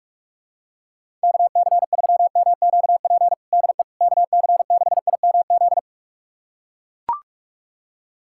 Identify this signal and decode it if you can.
Morse code, heard as a keyed tone, about 41 wpm, 700 Hz. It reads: KC3MQP DE KC6IMZ
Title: USA Callsign DE Callsign 45wpm